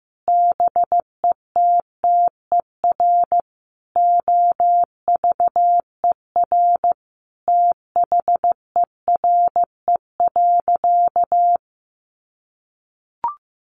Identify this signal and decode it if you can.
Morse code, heard as a keyed tone, about 15 wpm, 700 Hz. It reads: BETTER OVER THERE.
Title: Sentences from Top 1500 Words 15wpm